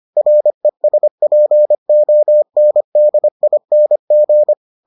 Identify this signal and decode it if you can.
Morse code, heard as a keyed tone, about 25 wpm, 600 Hz. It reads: RESPONDING